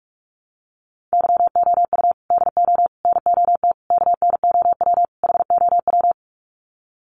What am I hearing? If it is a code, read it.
Morse code, heard as a keyed tone, about 44 wpm, 700 Hz. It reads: YOU DO NOT KNOW HOW
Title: Sets of 5 Words Encore 45wpm